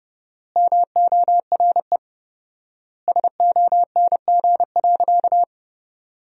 Morse code, heard as a keyed tone, about 32 wpm, 700 Hz. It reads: MORE SONG.
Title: 4x Word-Spacing - Sentences from Top 400 Words - 30wpm